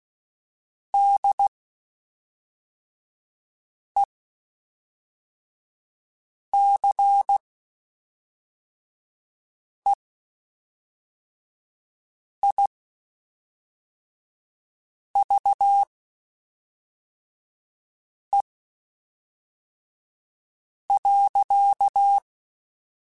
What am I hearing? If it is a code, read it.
Morse code, heard as a keyed tone, about 16 wpm, spaced as for 3 wpm, 800 Hz. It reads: DECEIVE.
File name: Quotes Spaced / Quotes Spaced-mp3s / Deceive sp @5